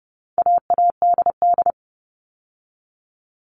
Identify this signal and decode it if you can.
Morse code, heard as a keyed tone, about 30 wpm, 700 Hz. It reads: AADD
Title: All Cut Numbers - 4-Digit 30wpm